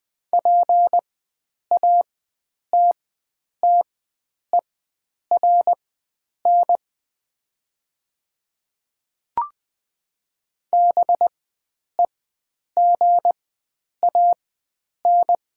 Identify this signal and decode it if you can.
Morse code, heard as a keyed tone, about 20 wpm, spaced as for 9 wpm, 700 Hz. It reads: PATTERN BEGAN
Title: 4x Character Spacing - Top 500 Words - 20wpm